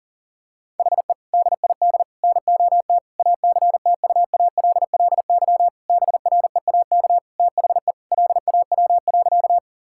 Morse code, heard as a keyed tone, about 40 wpm, 700 Hz. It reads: HE DID NOT ACTUALLY BREAK THE LAW.
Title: Sentences from Top 1000 Words - Review Part 6 - 40wpm